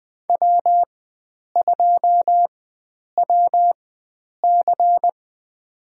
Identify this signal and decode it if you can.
Morse code, heard as a keyed tone, about 20 wpm, spaced as for 9 wpm, 700 Hz. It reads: W2WC
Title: USA Call Signs - 4 Characters - 4x Character Spacing - 20wpm